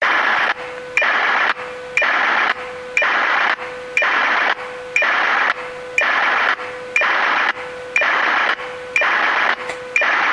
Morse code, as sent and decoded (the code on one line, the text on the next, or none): none